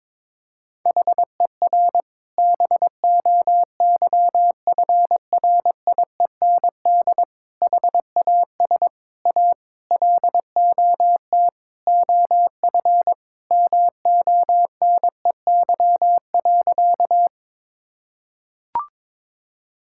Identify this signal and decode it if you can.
Morse code, heard as a keyed tone, about 22 wpm, 700 Hz. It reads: HER BOYFRIEND HAS A LOT OF MONEY.
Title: Sentences from Top 1000 Words - Review Part 5 - 22wpm